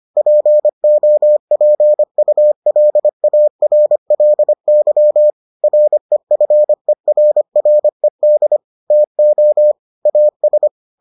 Morse code, heard as a keyed tone, about 25 wpm, 600 Hz. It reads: POPULARLY REFERRED TO AS